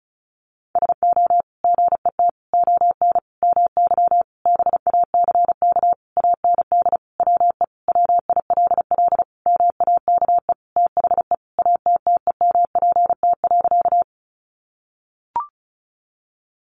Morse code, heard as a keyed tone, about 34 wpm, 700 Hz. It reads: SO GET ON MY BACK AND WE WILL MAKE THE ATTEMPT.
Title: Sentences from Top 1000 Words 35wpm